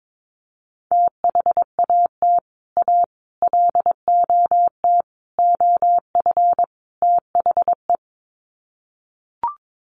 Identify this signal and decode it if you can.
Morse code, heard as a keyed tone, about 22 wpm, 700 Hz. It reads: THAT A LOT OF THE
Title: Sets of 5 Words Encore2 22wpm